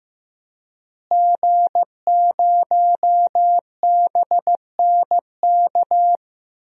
Morse code, heard as a keyed tone, about 15 wpm, 700 Hz. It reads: G0BNK
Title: DXCC-100 Call Signs - 5 Characters - Repeated 2x - 15wpm